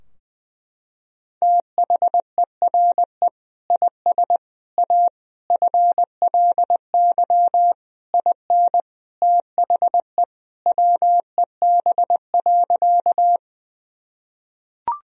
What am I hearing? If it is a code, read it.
Morse code, heard as a keyed tone, about 20 wpm, 700 Hz. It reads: THERE IS A FLY IN THE WEB.
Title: Sentences from Top 1000 Words - Review Part 4 - 20wpm